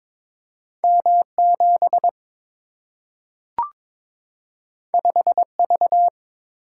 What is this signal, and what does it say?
Morse code, heard as a keyed tone, about 22 wpm, 700 Hz. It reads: M 7 5 V
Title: Single-Letter Single-Number 22wpm